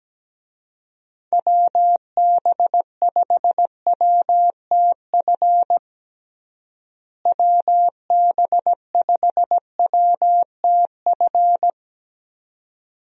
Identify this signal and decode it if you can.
Morse code, heard as a keyed tone, about 17 wpm, 700 Hz. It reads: WB5WTF WB5WTF
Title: USA Call Signs - 6 Characters - Repeated 2x - 17wpm